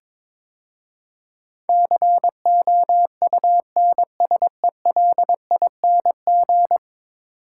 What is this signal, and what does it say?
Morse code, heard as a keyed tone, about 22 wpm, 700 Hz. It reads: COUNSELING
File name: practice / 22wpm / 10 Letter Words - First 3 Letters Given - 22wpm